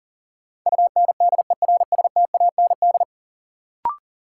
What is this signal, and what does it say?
Morse code, heard as a keyed tone, about 40 wpm, 700 Hz. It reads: UNDERSTAND
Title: Sentences from Top 600 Words - New Word Drill - 40wpm